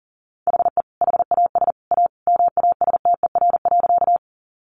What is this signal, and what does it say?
Morse code, heard as a keyed tone, about 40 wpm, 700 Hz. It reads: HE HAS A MASTER.
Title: Sentences from Top 1300 Words 40wpm